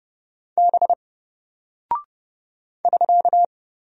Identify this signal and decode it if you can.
Morse code, heard as a keyed tone, about 30 wpm, 700 Hz. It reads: B <SK>
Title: Single Character (Letter:Number:Punctuation:Prosign) Alt 30wpm